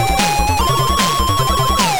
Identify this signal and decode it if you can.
There is no code.